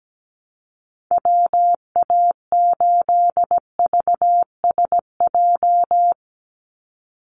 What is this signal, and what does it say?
Morse code, heard as a keyed tone, about 17 wpm, 700 Hz. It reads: WA8VSJ
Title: Call Signs - 17wpm